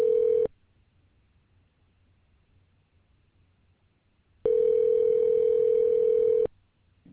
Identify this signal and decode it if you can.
Ringback tone (North America), 440 + 480 Hz.